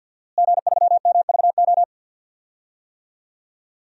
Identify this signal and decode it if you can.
Morse code, heard as a keyed tone, about 51 wpm, 700 Hz. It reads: G3MVO
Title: DXCC-100 Call Signs - 5 Characters - Repeated 2x - 50wpm